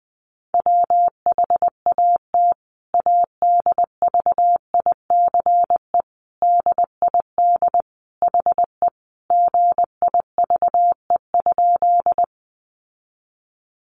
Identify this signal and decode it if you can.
Morse code, heard as a keyed tone, about 20 wpm, 700 Hz. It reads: WHAT ADVICE DID HE GIVE?
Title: Sentences from Top 1100 Words 20wpm